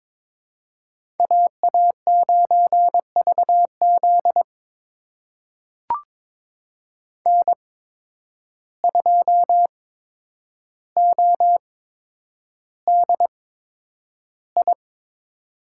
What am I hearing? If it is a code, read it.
Morse code, heard as a keyed tone, about 22 wpm, spaced as for 9 wpm, 700 Hz. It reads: AA9VZ N2ODI
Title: USA Call Signs - 8x Character Spacing - 22wpm